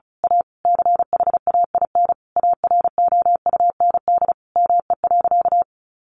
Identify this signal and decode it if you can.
Morse code, heard as a keyed tone, about 34 wpm, 700 Hz. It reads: A CHAIN AROUND ME.
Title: Sentences from Top 1500 Words 35wpm